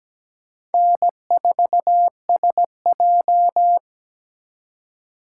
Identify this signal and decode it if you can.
Morse code, heard as a keyed tone, about 20 wpm, 700 Hz. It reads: N4SJ
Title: Speed Racing - USA 1x2 Call Signs 17wpm